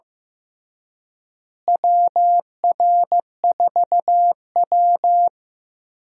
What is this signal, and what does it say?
Morse code, heard as a keyed tone, about 15 wpm, 700 Hz. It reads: WR4W
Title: USA Call Signs - 4 Characters - Repeated 3x - 15wpm